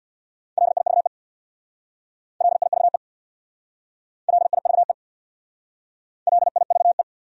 Morse code, heard as a keyed tone, about 59 wpm, 700 Hz. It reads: LIVE LIVE LIVE LIVE
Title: Speed-Racing Top 300 Words 50wpm